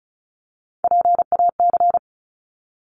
Morse code, heard as a keyed tone, about 34 wpm, 700 Hz. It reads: PAC
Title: ARRL Sections 35wpm